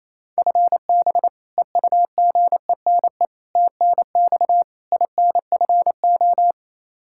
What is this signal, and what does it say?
Morse code, heard as a keyed tone, about 28 wpm, 700 Hz. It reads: FB EUGENE TNX INFO
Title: Practice QSO - Exchange 3 - More Info - 28wpm